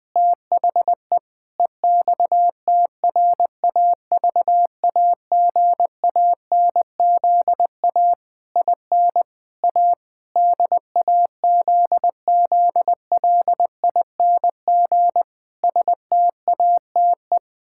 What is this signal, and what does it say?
Morse code, heard as a keyed tone, about 21 wpm, 700 Hz. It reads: THE EXTRAVAGANZA IN A DAZZLING STATE
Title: Pangram Challenge v2 - 20wpm